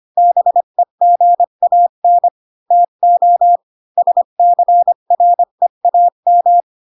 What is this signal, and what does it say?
Morse code, heard as a keyed tone, about 25 wpm, 700 Hz. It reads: BEGAN TO SCREAM
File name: Pinocchio / Pinocchio ch7 20 25 ik1hge